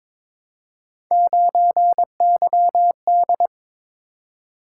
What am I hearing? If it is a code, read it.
Morse code, heard as a keyed tone, about 22 wpm, 700 Hz. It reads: 9YD
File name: practice / 22wpm / Three Letter-Number Combinations - 22wpm